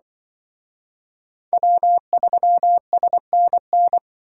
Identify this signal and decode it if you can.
Morse code, heard as a keyed tone, about 24 wpm, 700 Hz. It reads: W3SNN
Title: Speed Racing - USA 1x3 Call Signs 20wpm